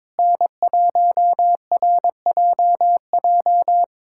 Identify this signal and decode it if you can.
Morse code, heard as a keyed tone, about 22 wpm, 700 Hz. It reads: N1RJJ
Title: USA Callsign DE Callsign - 5 Characters - 22wpm